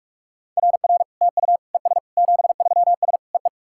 Morse code, heard as a keyed tone, about 46 wpm, 690 Hz. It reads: RR TU ES 73S EE
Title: SOTA QSO Practice 45wpm